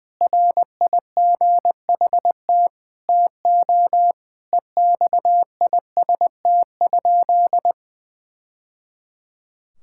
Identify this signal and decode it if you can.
Morse code, heard as a keyed tone, about 21 wpm, 700 Hz. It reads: RIGHT TO EXIST?
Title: Sentences from Top 1000 Words - Review Part 5 - 20wpm